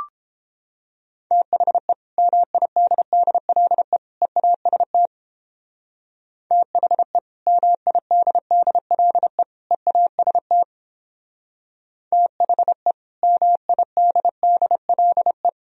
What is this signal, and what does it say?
Morse code, heard as a keyed tone, about 31 wpm, 700 Hz. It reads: THE MIDDLE EAST THE MIDDLE EAST THE MIDDLE